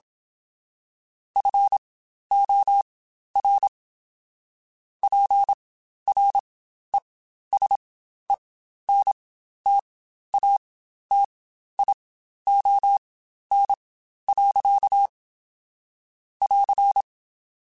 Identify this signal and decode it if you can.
Morse code, heard as a keyed tone, about 27 wpm, spaced as for 12 wpm, 800 Hz. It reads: FOR PRESENTATION. <AR>
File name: Quotes Spaced / Quotes Spaced-mp3s / Persperation sp @25